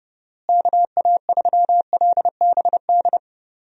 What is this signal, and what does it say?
Morse code, heard as a keyed tone, about 29 wpm, 700 Hz. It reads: KA3LBD
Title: USA Call Signs - 6 Characters - Repeated 2x - 30wpm